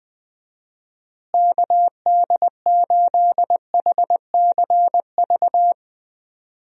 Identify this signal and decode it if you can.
Morse code, heard as a keyed tone, about 20 wpm, 700 Hz. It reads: KD8HCV
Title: USA Call Signs - 6 Characters - Repeated 2x - 20wpm